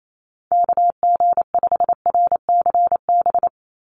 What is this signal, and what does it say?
Morse code, heard as a keyed tone, about 28 wpm, 700 Hz. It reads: KG5RCB
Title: USA Call Signs 28wpm